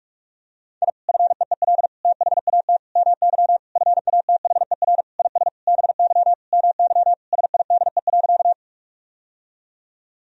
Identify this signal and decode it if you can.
Morse code, heard as a keyed tone, about 46 wpm, 700 Hz. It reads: I FEEL THAT MY FATHER IS BY MY SIDE.
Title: Sentences from Top 1000 Words - Review Part 1 - 45wpm